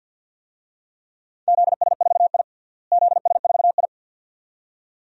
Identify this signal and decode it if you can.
Morse code, heard as a keyed tone, about 53 wpm, 700 Hz. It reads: ZS4I ZS4I
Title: DXCC-100 Call Signs - 4 Characters - Repeated 2x - 50wpm